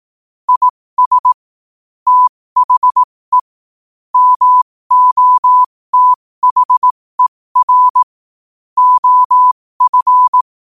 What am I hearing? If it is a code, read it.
Morse code, heard as a keyed tone, about 18 wpm, spaced as for 15 wpm, 1000 Hz. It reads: IS THE MOTHER OF
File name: qotdfiles / 15wpm